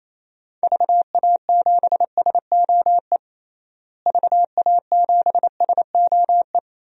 Morse code, heard as a keyed tone, about 27 wpm, 700 Hz. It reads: VA7SOE VA7SOE
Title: DXCC-100 Call Signs - 6 Characters - Repeated 2x - 28wpm